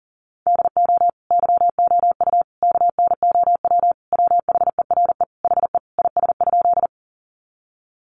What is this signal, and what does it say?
Morse code, heard as a keyed tone, about 40 wpm, 700 Hz. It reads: DO YOU KNOW WHERE HE IS?
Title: Sentences from Top 300 Words 40wpm